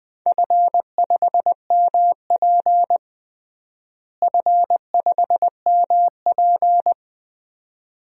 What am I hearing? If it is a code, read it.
Morse code, heard as a keyed tone, about 21 wpm, 700 Hz. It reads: F5MP F5MP
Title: DXCC-100 Call Signs - 4 Characters - Repeated 2x - 20wpm